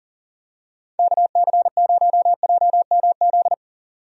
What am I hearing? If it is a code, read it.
Morse code, heard as a keyed tone, about 40 wpm, 700 Hz. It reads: KC0JMZ
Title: USA Call Signs - 6 Characters - 40wpm